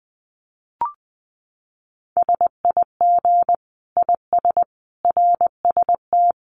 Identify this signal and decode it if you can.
Morse code, heard as a keyed tone, about 20 wpm, 700 Hz. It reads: SIG IS RST
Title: Common QSO Phrases 20wpm